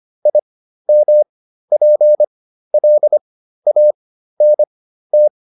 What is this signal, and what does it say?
Morse code, heard as a keyed tone, about 25 wpm, spaced as for 13 wpm, 600 Hz. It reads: IMPLANT
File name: IM7 13